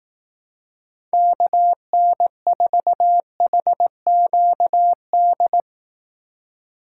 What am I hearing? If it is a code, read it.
Morse code, heard as a keyed tone, about 18 wpm, 700 Hz. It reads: KN4HQD
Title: Speed Racing - USA 2x3 Call Signs 15wpm